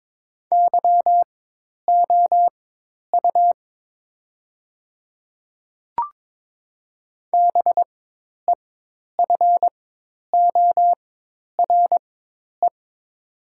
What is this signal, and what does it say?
Morse code, heard as a keyed tone, about 22 wpm, spaced as for 10 wpm, 700 Hz. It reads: YOU BEFORE
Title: 4x Character Spacing - Top 200 Words - 22wpm